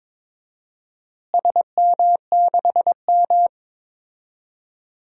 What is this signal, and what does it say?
Morse code, heard as a keyed tone, about 22 wpm, 700 Hz. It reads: SM6M
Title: DXCC-100 Call Signs - 4 Characters - Repeated 2x - 22wpm